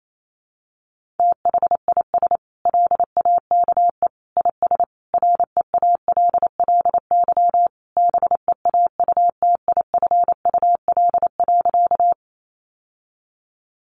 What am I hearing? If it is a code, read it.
Morse code, heard as a keyed tone, about 28 wpm, 700 Hz. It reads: THIS LAKE IS REALLY BEAUTIFUL.
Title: Sentences from Top 1100 Words 28wpm